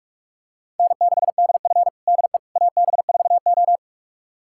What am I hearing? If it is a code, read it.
Morse code, heard as a keyed tone, about 46 wpm, 700 Hz. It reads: N6DF DE AB4O
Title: USA Callsign DE Callsign - 4 Characters - 45wpm